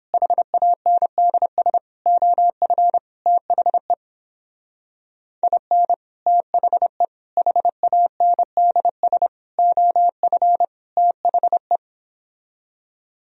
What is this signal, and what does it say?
Morse code, heard as a keyed tone, about 26 wpm, 700 Hz. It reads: HANDS OF THE IN THE HANDS OF THE
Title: Speed-Racing - Sets of 5 Words 22wpm